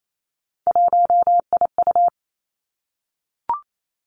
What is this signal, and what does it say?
Morse code, heard as a keyed tone, about 28 wpm, 700 Hz. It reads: 1IU
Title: Three Challenging Letter-Number Combinations 28wpm